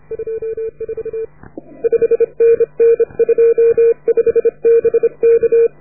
Morse code, heard as a keyed tone, about 30 wpm, 440 Hz. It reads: 24 5NN 25 BK